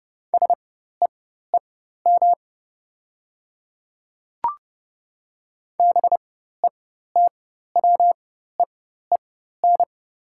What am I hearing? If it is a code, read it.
Morse code, heard as a keyed tone, about 30 wpm, spaced as for 14 wpm, 690 Hz. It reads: SEEM BETWEEN
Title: 4x Character Spacing - Top 500 Words - 30wpm